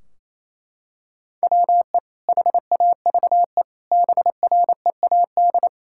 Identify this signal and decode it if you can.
Morse code, heard as a keyed tone, about 28 wpm, 700 Hz. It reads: WE HAVE BREAD
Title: Sentences from Top 1600 Words 28wpm